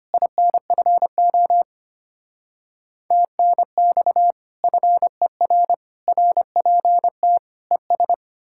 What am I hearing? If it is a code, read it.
Morse code, heard as a keyed tone, about 33 wpm, 700 Hz. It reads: INFO TNX FER RPT ES